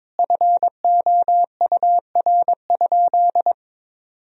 Morse code, heard as a keyed tone, about 22 wpm, 700 Hz. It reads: FOUR?
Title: Sentences from Top 300 Words and Abbreviations 22wpm